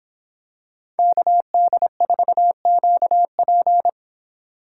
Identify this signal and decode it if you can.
Morse code, heard as a keyed tone, about 26 wpm, 700 Hz. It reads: KD4QP
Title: Speed Racing - USA 2x2 Call Signs 22wpm